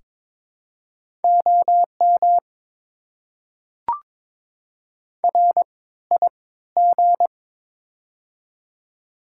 Morse code, heard as a keyed tone, about 22 wpm, spaced as for 14 wpm, 690 Hz. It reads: OM RIG
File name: Top 90 QSO Elements - 3x Character Spacing - 22wpm